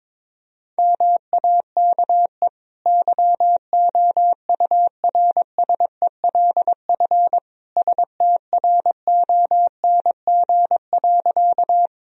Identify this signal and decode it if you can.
Morse code, heard as a keyed tone, about 21 wpm, 700 Hz. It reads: MAKE YOURSELF STRONG.
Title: Sentences from Top 1000 Words - Review Part 3 - 22wpm